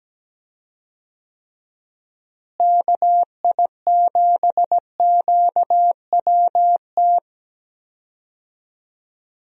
Morse code, heard as a keyed tone, about 17 wpm, 700 Hz. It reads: KI7QWT
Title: USA Call Signs - 6 Characters - Repeated 2x - 17wpm